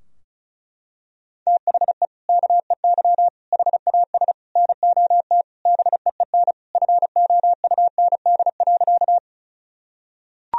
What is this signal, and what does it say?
Morse code, heard as a keyed tone, about 36 wpm, 700 Hz. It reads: THE KEY HAS NOT BEEN FOUND.
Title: Sentences from Top 1000 Words - Review Part 4 - 35wpm